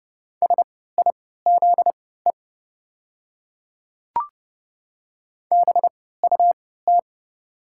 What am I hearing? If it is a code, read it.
Morse code, heard as a keyed tone, about 30 wpm, spaced as for 17 wpm, 700 Hz. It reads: SIZE BUT
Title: 3x Character Spacing - Top 500 Words - 30wpm